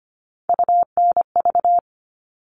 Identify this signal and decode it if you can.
Morse code, heard as a keyed tone, about 25 wpm, 700 Hz. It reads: UNV
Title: All Cut Numbers - 3-Digit 25wpm